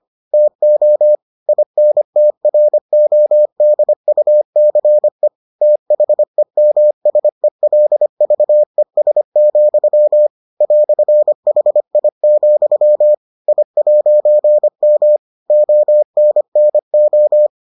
Morse code, heard as a keyed tone, about 25 wpm, 600 Hz. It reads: TO INTRODUCE THEMSELVES, "HI, I'M ONNO